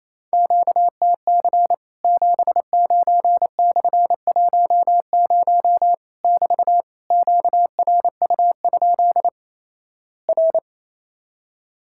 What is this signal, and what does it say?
Morse code, heard as a keyed tone, about 28 wpm, 700 Hz. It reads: QTC 79/10 <BT> QRU? R
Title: Worked All Europe DX Contest - QTC Exchange - 28wpm